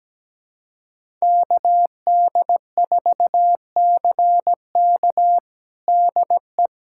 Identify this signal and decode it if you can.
Morse code, heard as a keyed tone, about 17 wpm, 700 Hz. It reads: KD4CK DE
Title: USA Callsign DE Callsign - 5 Characters - 17wpm